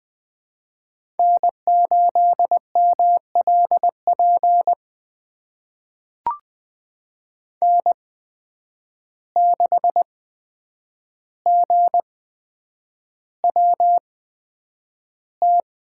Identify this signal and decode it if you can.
Morse code, heard as a keyed tone, about 20 wpm, spaced as for 9 wpm, 700 Hz. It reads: N8MLP N6GWT
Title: USA Call Signs - 5 Characters - 8x Character Spacing - 20wpm